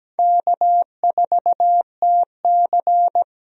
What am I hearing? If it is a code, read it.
Morse code, heard as a keyed tone, about 17 wpm, 700 Hz. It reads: K4TC
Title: USA Call Signs - 4 Characters - 17wpm